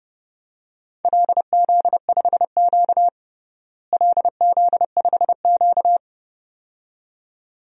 Morse code, heard as a keyed tone, about 30 wpm, 700 Hz. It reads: LZ5Q LZ5Q
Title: DXCC-100 Call Signs - 4 Characters - Repeated 2x - 30wpm